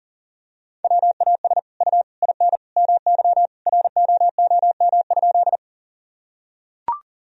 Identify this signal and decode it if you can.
Morse code, heard as a keyed tone, about 40 wpm, 700 Hz. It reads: WAS U IN MY ROOM?
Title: Sentences from Top 300 Words and Abbreviations 40wpm